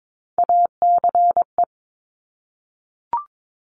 Morse code, heard as a keyed tone, about 22 wpm, 700 Hz. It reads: ACE
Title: Top Three Letter Combinations in Words 22wpm